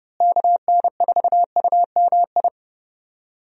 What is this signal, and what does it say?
Morse code, heard as a keyed tone, about 31 wpm, 700 Hz. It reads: KN4UMI
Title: USA Call Signs - 6 Characters - Repeated 3x - 30wpm